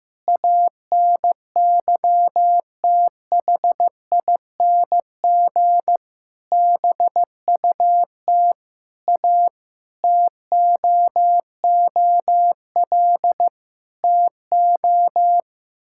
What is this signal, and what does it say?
Morse code, heard as a keyed tone, about 15 wpm, 700 Hz. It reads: ANYTHING BUT A TOOL TO
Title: Sentences from Top 1000 Words - Review Part 6 - 15wpm